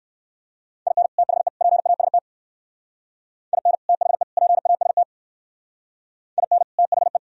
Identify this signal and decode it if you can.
Morse code, heard as a keyed tone, about 59 wpm, 700 Hz. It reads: IN THE PAST IN THE PAST IN THE